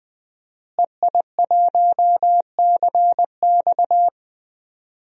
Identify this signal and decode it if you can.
Morse code, heard as a keyed tone, about 20 wpm, 700 Hz. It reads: EI1CX
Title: DXCC-100 Call Signs - 5 Characters - Repeated 2x - 20wpm